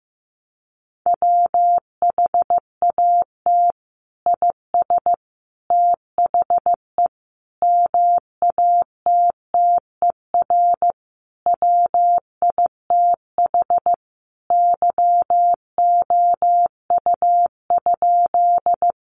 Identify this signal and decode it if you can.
Morse code, heard as a keyed tone, about 15 wpm, 700 Hz. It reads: WHAT IS THE MATTER WITH YOU?
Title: Sentences from Top 700 Words 15wpm